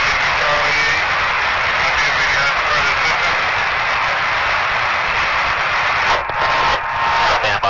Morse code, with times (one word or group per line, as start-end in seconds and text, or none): none